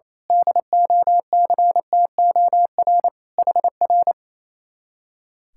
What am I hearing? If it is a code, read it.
Morse code, heard as a keyed tone, about 28 wpm, 700 Hz. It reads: DOCTOR HR
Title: Speed-Racing Common QSO Phrases Encore - 28wpm